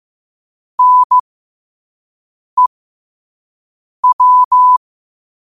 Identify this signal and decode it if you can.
Morse code, heard as a keyed tone, about 15 wpm, spaced as for 5 wpm, 1000 Hz. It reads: NEW
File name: qotdfiles / 5wpm